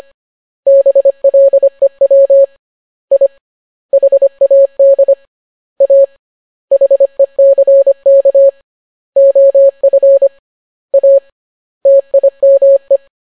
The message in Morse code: BLEW I HAD A HECK OF A TIME